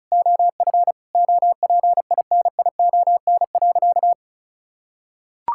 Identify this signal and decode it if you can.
Morse code, heard as a keyed tone, about 34 wpm, 700 Hz. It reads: OF OPINION.
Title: Sentences from Top 1600 Words 35wpm